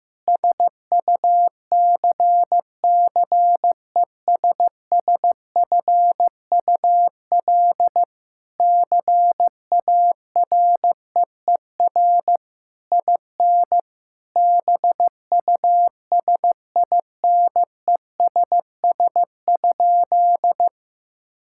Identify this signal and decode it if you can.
Morse code, heard as a keyed tone, about 15 wpm, 700 Hz. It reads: SUCCESSFUL CAREER IN BUSINESS?
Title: Sentences from Top 1000 Words - Review Part 6 - 15wpm